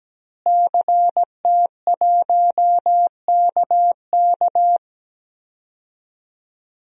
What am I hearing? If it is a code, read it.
Morse code, heard as a keyed tone, about 17 wpm, 700 Hz. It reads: CT1KK
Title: DXCC-100 Call Signs - 5 Characters - Repeated 3x - 17wpm